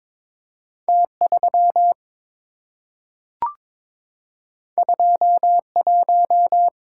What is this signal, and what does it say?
Morse code, heard as a keyed tone, about 22 wpm, 690 Hz. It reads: T3 21
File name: practice / 22wpm / Two Letter-Number Combinations - 22wpm